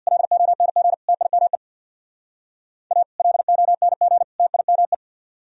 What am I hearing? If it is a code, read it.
Morse code, heard as a keyed tone, about 59 wpm, 700 Hz. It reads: LONG TIME A LONG TIME
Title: Speed-Racing - Sets of 3 Words 50wpm